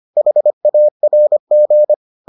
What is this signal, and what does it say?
Morse code, heard as a keyed tone, about 25 wpm, 600 Hz. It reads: HARG